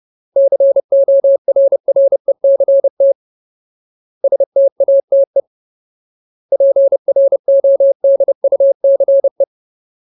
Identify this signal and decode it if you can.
Morse code, heard as a keyed tone, about 30 wpm, 550 Hz. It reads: CORRECT STATE PRODUCE
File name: Ultimate Word List / 30WPM Ultimate